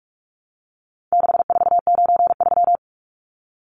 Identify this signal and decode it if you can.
Morse code, heard as a keyed tone, about 44 wpm, 700 Hz. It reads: 6493
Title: Numbers - 4-Digit 45wpm